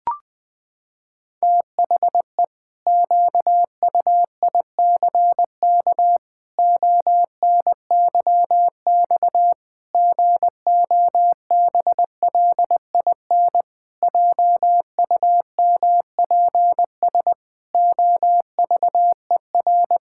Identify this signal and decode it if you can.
Morse code, heard as a keyed tone, about 20 wpm, 700 Hz. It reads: THE QUICK ONYX GOBLIN JUMPS OVER
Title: Pangram Challenge v2 - 20wpm